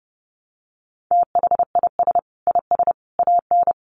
Morse code, heard as a keyed tone, about 30 wpm, 700 Hz. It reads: THIS IS AN
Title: Sentences from Top 900 Words 30wpm